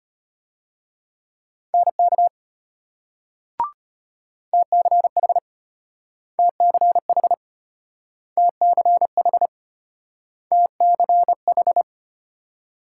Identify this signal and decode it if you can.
Morse code, heard as a keyed tone, about 34 wpm, 700 Hz. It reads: NK TCH TCH TCH TCH